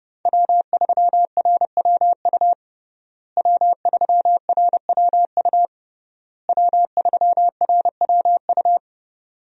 Morse code, heard as a keyed tone, about 32 wpm, 700 Hz. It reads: W3RWU W3RWU W3RWU
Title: USA Call Signs - 5 Characters - Repeated 3x - 30wpm